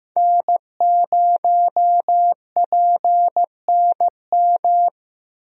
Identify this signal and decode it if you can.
Morse code, heard as a keyed tone, about 15 wpm, 700 Hz. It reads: N0PNM
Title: USA Call Signs - 5 Characters - Repeated 2x - 15wpm